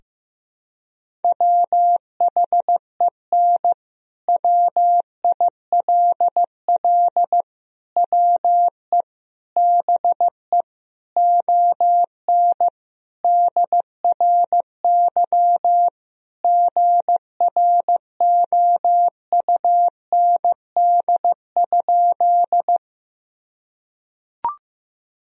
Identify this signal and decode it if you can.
Morse code, heard as a keyed tone, about 16 wpm, 700 Hz. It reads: WHEN WILL WE BE ON DRY GROUND?
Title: Sentences from Top 1000 Words - Review Part 3 - 15wpm